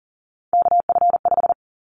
Morse code, heard as a keyed tone, about 41 wpm, 700 Hz. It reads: KF5
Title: Callsign Prefixes for Top 25 DXCC - 40wpm